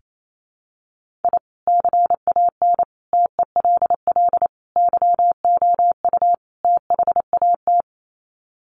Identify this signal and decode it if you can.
Morse code, heard as a keyed tone, about 28 wpm, 700 Hz. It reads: I CAN TELL YOU THAT
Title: Sets of 5 Words 28wpm